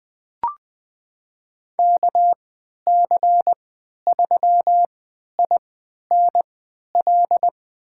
Morse code, heard as a keyed tone, about 20 wpm, spaced as for 11 wpm, 700 Hz. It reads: KC3INL
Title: USA Call Signs - 6 Characters - 3x Character Spacing - 20wpm